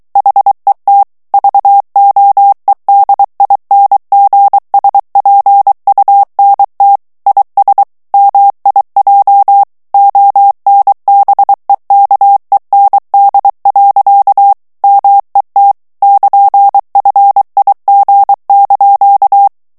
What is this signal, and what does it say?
Morse code, heard as a keyed tone, about 23 wpm, 800 Hz. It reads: HET VOEDINGSPUNT IS MIJ ONBEKEND. MET <KN>FIG)
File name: HOLLANDAIS texte clair 1 vitesse 22 mots minute